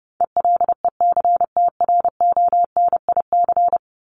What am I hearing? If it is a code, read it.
Morse code, heard as a keyed tone, about 30 wpm, 700 Hz. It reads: ELECTRONIC
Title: Top 1001-1250 Words 30wpm